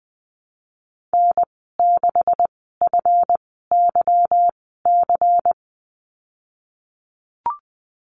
Morse code, heard as a keyed tone, about 20 wpm, spaced as for 14 wpm, 700 Hz. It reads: N6FYC
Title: 154 Lesson 25g - Letter p - Call Signs 20wpm